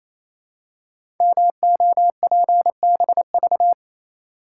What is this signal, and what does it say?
Morse code, heard as a keyed tone, about 27 wpm, 700 Hz. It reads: MOPBV
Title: Code Groups - 5 Letters - 28wpm